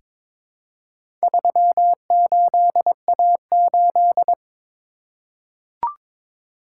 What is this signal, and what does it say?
Morse code, heard as a keyed tone, about 22 wpm, 690 Hz. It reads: 38A8
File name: practice / 22wpm / Four Letter-Number Combinations - 22wpm